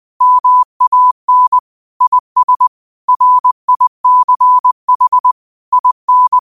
Morse code, heard as a keyed tone, about 20 wpm, 1000 Hz. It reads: MAN IS RICH IN
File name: qotdfiles / 20wpm